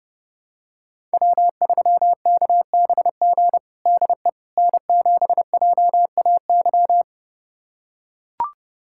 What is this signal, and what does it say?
Morse code, heard as a keyed tone, about 32 wpm, 700 Hz. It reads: W3KBG DE N7JAY
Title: USA Callsign DE Callsign - 5 Characters - 30wpm